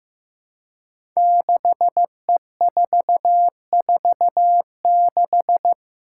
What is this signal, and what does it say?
Morse code, heard as a keyed tone, about 15 wpm, 700 Hz. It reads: 6E446
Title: Five Challenging Letter-Number Combinations 15wpm